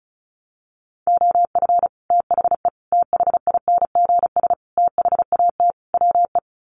Morse code, heard as a keyed tone, about 34 wpm, 700 Hz. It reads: OF THE THINGS THAT WE
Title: Sets of 5 Words Encore 35wpm